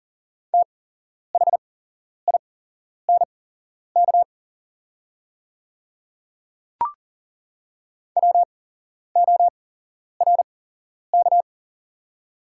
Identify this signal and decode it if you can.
Morse code, heard as a keyed tone, about 38 wpm, spaced as for 11 wpm, 690 Hz. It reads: THINK WORK
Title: 8x Character Spacing - Top 200 Words - 40wpm